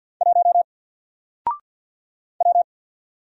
Morse code, heard as a keyed tone, about 50 wpm, 700 Hz. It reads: 1 W
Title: Single Letter-Number Alt 50wpm